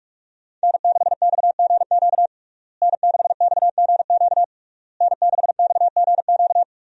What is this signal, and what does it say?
Morse code, heard as a keyed tone, about 46 wpm, 700 Hz. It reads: N6XGQ N6XGQ N6XGQ
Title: USA Call Signs - 5 Characters - Repeated 3x - 45wpm